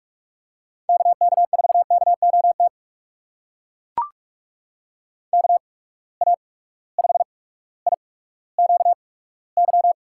Morse code, heard as a keyed tone, about 43 wpm, spaced as for 19 wpm, 700 Hz. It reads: KK4KOT KA5IQY
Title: USA Call Signs - 6 Characters - 8x Character Spacing - 45wpm